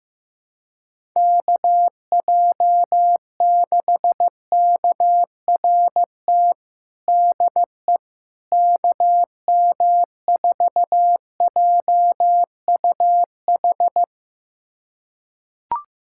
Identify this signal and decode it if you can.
Morse code, heard as a keyed tone, about 15 wpm, 700 Hz. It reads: KJ6KRT DE KM4JUH
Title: USA Callsign DE Callsign 15wpm